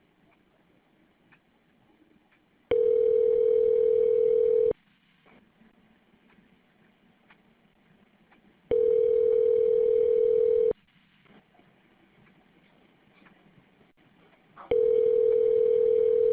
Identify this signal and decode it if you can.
Ringback tone (North America), 440 + 480 Hz.